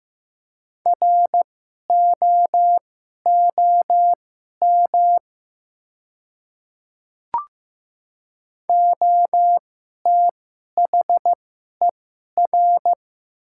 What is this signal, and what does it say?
Morse code, heard as a keyed tone, about 15 wpm, spaced as for 11 wpm, 700 Hz. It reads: ROOM OTHER
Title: 2x Character Spacing - Top 300 Words - 15wpm